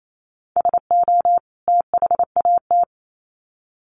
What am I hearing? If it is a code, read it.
Morse code, heard as a keyed tone, about 28 wpm, 700 Hz. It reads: SO THAT
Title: Sets of 2 Words 28wpm